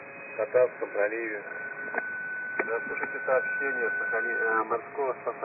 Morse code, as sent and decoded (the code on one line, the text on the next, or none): none